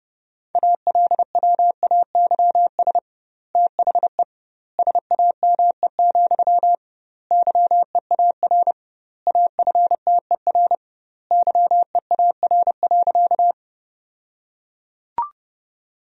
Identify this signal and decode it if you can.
Morse code, heard as a keyed tone, about 30 wpm, 700 Hz. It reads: ALWAYS THE SAME, YEAR AFTER YEAR.
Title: 2x Word-Spacing - Sentences from Top 300 Words - 30wpm